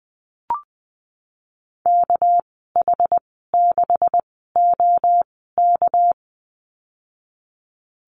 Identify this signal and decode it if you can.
Morse code, heard as a keyed tone, about 20 wpm, spaced as for 15 wpm, 700 Hz. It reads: KH6OK
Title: 219 Lesson 35g - Number 0 - Call Signs 20wpm